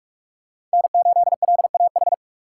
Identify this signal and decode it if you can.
Morse code, heard as a keyed tone, about 44 wpm, 690 Hz. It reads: N8LAH
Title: USA Call Signs - 5 Characters - 6x Character Spacing - 45wpm